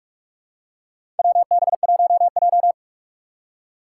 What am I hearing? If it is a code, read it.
Morse code, heard as a keyed tone, about 42 wpm, 700 Hz. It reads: WB1J
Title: USA Call Signs - 4 Characters - 2x Character Spacing - 45wpm